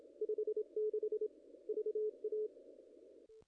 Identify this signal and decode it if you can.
Morse code, heard as a keyed tone, about 26 wpm, 440 Hz. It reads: 56 VA
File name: contests / ss-cw-k5tr-op-n5rz -0000